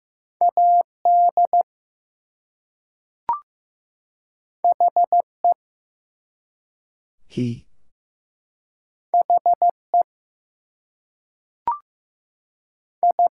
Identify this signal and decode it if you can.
Morse code, heard as a keyed tone, about 15 wpm, 700 Hz. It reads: AD HE HE I
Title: Two Letter Words - 15wpm